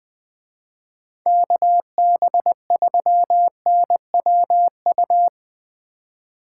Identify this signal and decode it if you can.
Morse code, heard as a keyed tone, about 20 wpm, 700 Hz. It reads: KB3NWU
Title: Speed Racing - USA 2x3 Call Signs 17wpm